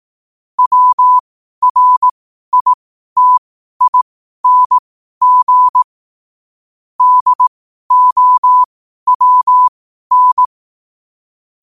Morse code, heard as a keyed tone, about 18 wpm, spaced as for 12 wpm, 1000 Hz. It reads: WRITING DOWN
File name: qotdfiles / 12wpm